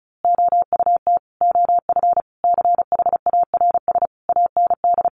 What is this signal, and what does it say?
Morse code, heard as a keyed tone, about 34 wpm, 700 Hz. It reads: OUT OF CHARS AND
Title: A Sound Of Thunder 35wpm